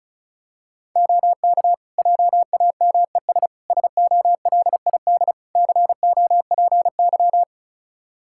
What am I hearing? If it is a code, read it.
Morse code, heard as a keyed tone, about 36 wpm, 700 Hz. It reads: OK JAMES SOLID COPY
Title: Practice QSO - Exchange 3 - More Info - 35wpm